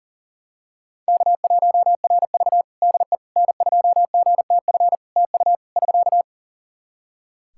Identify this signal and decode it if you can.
Morse code, heard as a keyed tone, about 37 wpm, 700 Hz. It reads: K1RV DE N2GTF TU <SK>
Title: Practice QSO - Exchange 4 - Ending - 40wpm